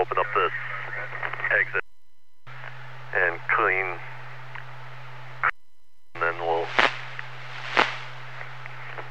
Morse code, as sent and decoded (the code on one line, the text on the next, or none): none